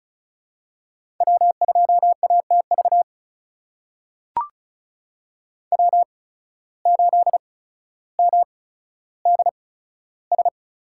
Morse code, heard as a keyed tone, about 36 wpm, spaced as for 15 wpm, 700 Hz. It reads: W2ATV W8MDS
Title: USA Call Signs - 8x Character Spacing - 35wpm